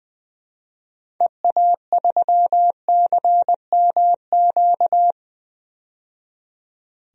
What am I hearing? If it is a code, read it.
Morse code, heard as a keyed tone, about 20 wpm, 700 Hz. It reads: EA3CMQ
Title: DXCC-100 Call Signs - 6 Characters - 20wpm